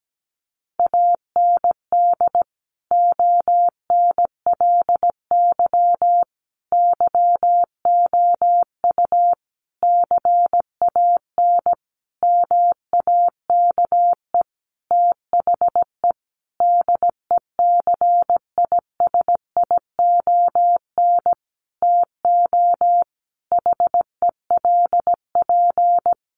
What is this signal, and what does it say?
Morse code, heard as a keyed tone, about 17 wpm, 700 Hz. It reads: AND ONLY YOU CAN MAKE THE DECISION TO HELP
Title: Sentences from Top 1400 Words 17wpm